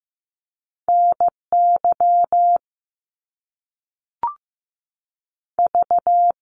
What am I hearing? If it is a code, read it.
Morse code, heard as a keyed tone, about 15 wpm, 700 Hz. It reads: NY V
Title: US States 15wpm